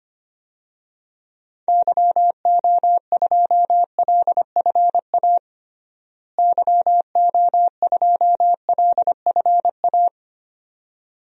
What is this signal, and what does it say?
Morse code, heard as a keyed tone, about 25 wpm, 700 Hz. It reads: YO2LFA YO2LFA
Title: DXCC-100 Call Signs - 6 Characters - Repeated 2x - 25wpm